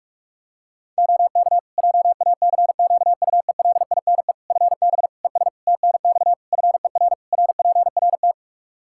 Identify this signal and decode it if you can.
Morse code, heard as a keyed tone, about 41 wpm, 700 Hz. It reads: OK JACQUELINE FB ES TNX FER RPRT
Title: K1USN SST QSO Practice 45wpm